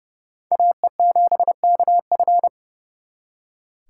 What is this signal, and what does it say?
Morse code, heard as a keyed tone, about 37 wpm, 700 Hz. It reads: AE7KF
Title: Speed Racing - USA 2x2 Call Signs 30wpm